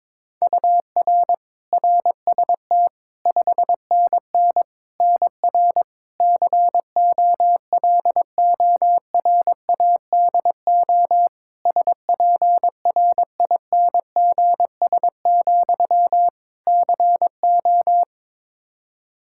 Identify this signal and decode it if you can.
Morse code, heard as a keyed tone, about 22 wpm, 700 Hz. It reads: UR RST 5NN NR COLORADO SPRINGS, CO
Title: Practice QSO - Exchange 1 - Beginning - 22wpm